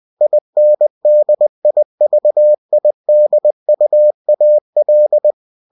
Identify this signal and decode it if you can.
Morse code, heard as a keyed tone, about 20 wpm, 600 Hz. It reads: INDIVIDUAL
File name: cw-books / winnie-the-pooh / winnie-20wpm-page-0115